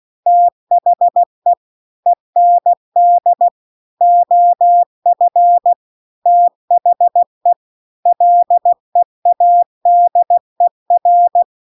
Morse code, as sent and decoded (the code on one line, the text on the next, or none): - .... / . -. -.. / --- ..-. / - .... / .-.. .- -.. .-.
THE END OF THE LEADER